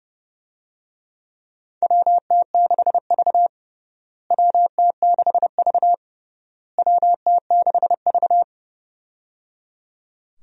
Morse code, heard as a keyed tone, about 29 wpm, 700 Hz. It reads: WT6V WT6V WT6V
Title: USA Call Signs - 4 Characters - Repeated 3x - 30wpm